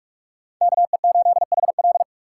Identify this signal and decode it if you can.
Morse code, heard as a keyed tone, about 39 wpm, 700 Hz. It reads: KE8HL